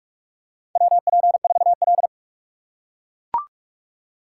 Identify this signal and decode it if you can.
Morse code, heard as a keyed tone, about 44 wpm, 700 Hz. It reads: WP4L
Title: USA Call Signs - 4 Characters - 3x Character Spacing - 45wpm